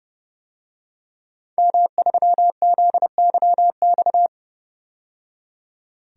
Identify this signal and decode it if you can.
Morse code, heard as a keyed tone, about 30 wpm, 700 Hz. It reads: M3ZYX
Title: DXCC-100 Call Signs - 5 Characters - 30wpm